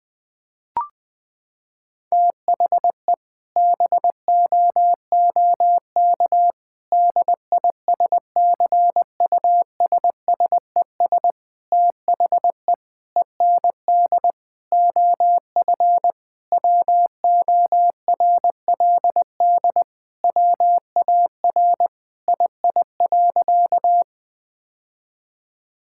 Morse code, heard as a keyed tone, about 20 wpm, 700 Hz. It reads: THE BOOK DISCUSSES THE END OF WORLD WAR II.
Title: Sentences with 6-Word Phrases - 20wpm